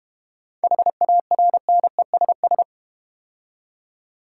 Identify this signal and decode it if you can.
Morse code, heard as a keyed tone, about 30 wpm, 690 Hz. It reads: HARNESS
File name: practice / 25wpm / Difficult Words - Speed-Racing with Hints - 25wpm